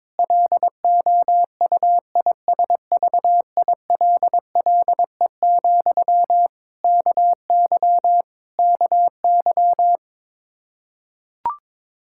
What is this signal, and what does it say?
Morse code, heard as a keyed tone, about 22 wpm, 700 Hz. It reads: LOUISVILLE, KY KY
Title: Practice QSO - Exchange 1 - Beginning - 22wpm